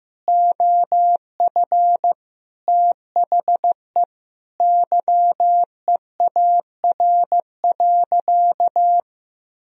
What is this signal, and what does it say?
Morse code, heard as a keyed tone, about 15 wpm, 700 Hz. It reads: OF THE YEAR.
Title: Sentences from Top 1000 Words - Review Part 1 - 15wpm